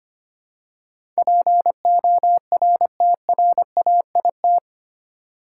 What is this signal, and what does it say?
Morse code, heard as a keyed tone, about 25 wpm, 700 Hz. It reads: PORTRAIT
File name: 8 Letter Words - First 3 Letters Given - 25wpm